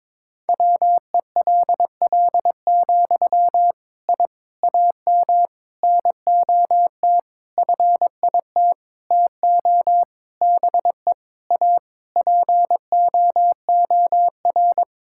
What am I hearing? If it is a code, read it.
Morse code, heard as a keyed tone, about 21 wpm, 700 Hz. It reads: WELL, I AM NOT FIT TO BE A POOR
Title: Sentences from Top 1000 Words - Review Part 5 - 22wpm